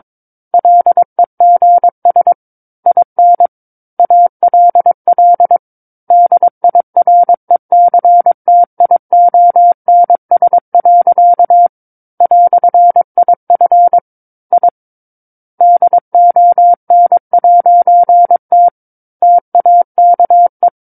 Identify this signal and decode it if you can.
Morse code, heard as a keyed tone, about 22 wpm, 700 Hz. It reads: LEGS IN ALL DIRECTIONS. "IF I DON'T TAKE